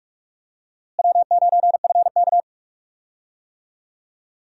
Morse code, heard as a keyed tone, about 42 wpm, 700 Hz. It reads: W9FK
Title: USA Call Signs - 4 Characters - Repeated 2x - 45wpm